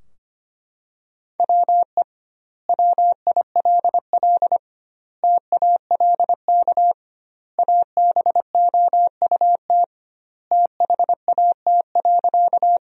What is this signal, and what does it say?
Morse code, heard as a keyed tone, about 25 wpm, 700 Hz. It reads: WE WILL TALK ABOUT THAT.
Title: 2x Word-Spacing - Sentences from Top 400 Words - 25wpm